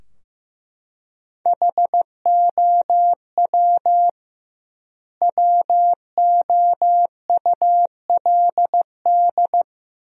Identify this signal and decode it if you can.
Morse code, heard as a keyed tone, about 15 wpm, 700 Hz. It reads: HOW WOULD
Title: 2x Word-Spacing - Sentences from Top 100 Words - 15wpm